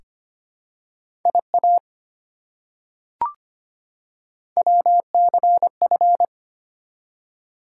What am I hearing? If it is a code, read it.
Morse code, heard as a keyed tone, about 25 wpm, 700 Hz. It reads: IA WCF
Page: morse-code-ninja-practice-25wpm